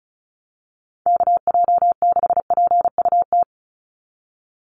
Morse code, heard as a keyed tone, about 34 wpm, 700 Hz. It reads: KJ6PUT
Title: USA Call Signs 35wpm